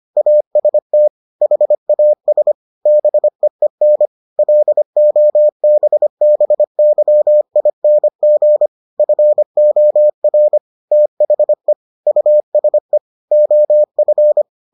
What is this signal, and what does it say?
Morse code, heard as a keyed tone, about 25 wpm, 600 Hz. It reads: AST HAS BEEN LOBBYING FOR THE USE OF